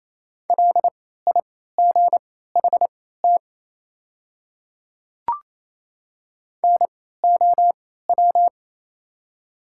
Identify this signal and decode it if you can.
Morse code, heard as a keyed tone, about 28 wpm, spaced as for 16 wpm, 690 Hz. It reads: LIGHT NOW
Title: 3x Character Spacing - Top 200 Words - 28wpm